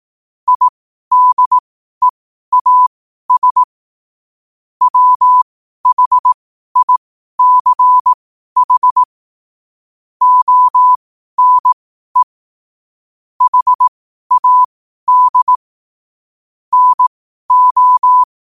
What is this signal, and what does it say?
Morse code, heard as a keyed tone, about 18 wpm, spaced as for 12 wpm, 1000 Hz. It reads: IDEAS WHICH ONE HAD NO
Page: New quotes every day in morse code at 12 Words per minute.